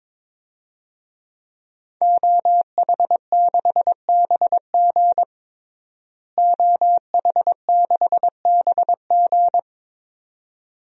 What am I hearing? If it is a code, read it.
Morse code, heard as a keyed tone, about 22 wpm, 700 Hz. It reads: OH6BG OH6BG
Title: DXCC-100 Call Signs - 5 Characters - Repeated 2x - 22wpm